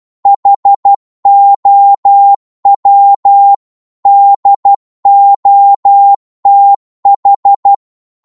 Morse code, heard as a keyed tone, about 12 wpm, 800 Hz. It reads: HOW DOTH